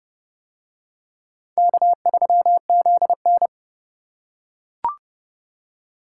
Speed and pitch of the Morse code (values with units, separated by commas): 26 wpm, 700 Hz